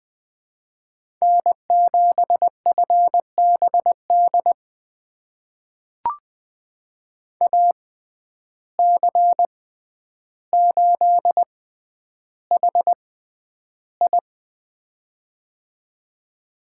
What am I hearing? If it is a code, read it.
Morse code, heard as a keyed tone, about 20 wpm, spaced as for 10 wpm, 700 Hz. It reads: N7FBD AC8HI
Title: USA Call Signs - 5 Characters - 6x Character Spacing - 20wpm